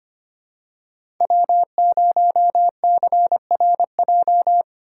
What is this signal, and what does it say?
Morse code, heard as a keyed tone, about 25 wpm, 700 Hz. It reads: W0CRJ
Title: USA Call Signs - 3x Character Spacing - 25wpm